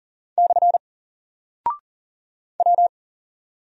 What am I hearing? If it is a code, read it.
Morse code, heard as a keyed tone, about 40 wpm, 700 Hz. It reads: / W
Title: Single Character (Letter:Number:Punctuation:Prosign) Alt 40wpm